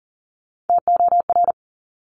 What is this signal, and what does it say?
Morse code, heard as a keyed tone, about 40 wpm, 700 Hz. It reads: TOR
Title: Top Three Letter Combinations in Words 40wpm